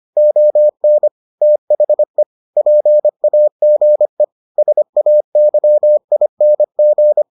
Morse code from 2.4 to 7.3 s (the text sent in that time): PAGE SAYING